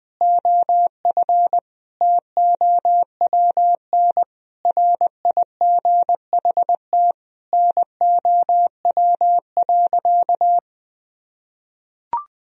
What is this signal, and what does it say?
Morse code, heard as a keyed tone, about 20 wpm, 700 Hz. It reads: OF TOWN RIGHT NOW.
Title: Sentences from Top 1000 Words - Review Part 4 - 20wpm